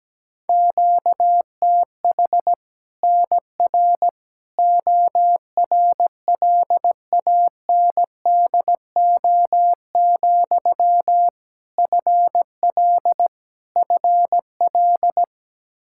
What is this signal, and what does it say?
Morse code, heard as a keyed tone, about 17 wpm, 700 Hz. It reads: QTH NR ORLANDO, FL FL
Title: Practice QSO - Exchange 1 - Beginning - 17wpm